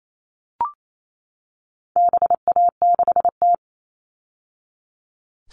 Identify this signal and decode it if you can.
Morse code, heard as a keyed tone, about 28 wpm, 700 Hz. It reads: BA6T
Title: All Cut Numbers - 4-Digit 28wpm